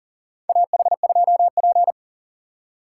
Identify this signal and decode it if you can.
Morse code, heard as a keyed tone, about 37 wpm, 700 Hz. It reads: AH2P
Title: USA Call Signs - 2x Character Spacing - 40wpm